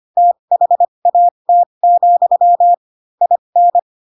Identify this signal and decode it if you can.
Morse code, heard as a keyed tone, about 25 wpm, spaced as for 22 wpm, 700 Hz. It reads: THAT, IN
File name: Pinocchio / Pinocchio ch6 18 25 ik1hge